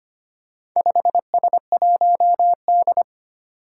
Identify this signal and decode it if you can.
Morse code, heard as a keyed tone, about 25 wpm, 700 Hz. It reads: 5S1D
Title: Four Challenging Letter-Number Combinations 25wpm